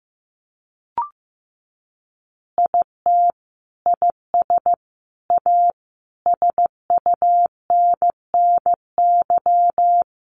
Morse code, heard as a keyed tone, about 15 wpm, 700 Hz. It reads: IT IS A SUNNY
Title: Easy Phrases 15wpm